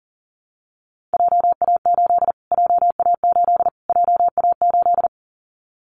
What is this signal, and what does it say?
Morse code, heard as a keyed tone, about 40 wpm, 700 Hz. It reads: JA8 JA8 JA8
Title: Callsign Prefixes for Top 25 DXCC - Repeated 3x - 40wpm